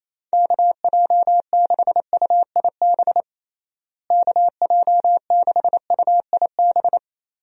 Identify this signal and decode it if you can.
Morse code, heard as a keyed tone, about 28 wpm, 700 Hz. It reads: KJ6UIB KJ6UIB
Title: USA Call Signs - 6 Characters - Repeated 3x - 28wpm